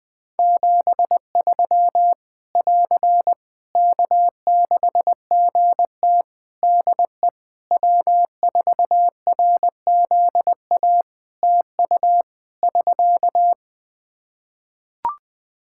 Morse code, heard as a keyed tone, about 20 wpm, 700 Hz. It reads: 73 <AR> K6GT DE W4RZA TU <SK>
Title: Practice QSO - Exchange 4 - Ending - 20wpm